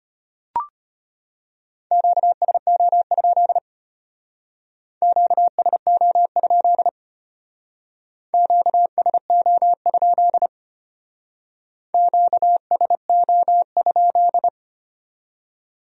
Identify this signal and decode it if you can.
Morse code, heard as a keyed tone, about 31 wpm, 700 Hz. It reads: QSO? QSO? QSO? QSO?
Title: Speed-Racing Q-Codes - 25wpm